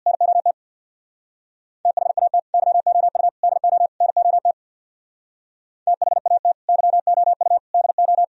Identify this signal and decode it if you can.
Morse code, heard as a keyed tone, about 64 wpm, 700 Hz. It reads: NOT THAT YOU DO NOT THAT YOU DO